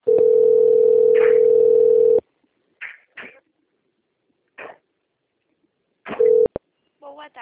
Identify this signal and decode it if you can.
Ringback tone (North America), 440 + 480 Hz.